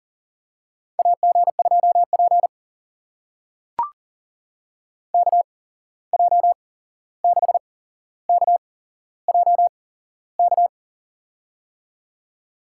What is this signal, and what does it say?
Morse code, heard as a keyed tone, about 38 wpm, spaced as for 15 wpm, 700 Hz. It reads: AG2P KJ6KJK
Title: USA Call Signs - 8x Character Spacing - 40wpm